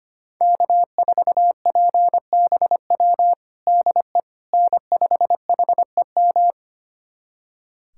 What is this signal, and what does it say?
Morse code, heard as a keyed tone, about 25 wpm, 700 Hz. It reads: K4PBW DE N5HEM
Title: USA Callsign DE Callsign - 5 Characters - 25wpm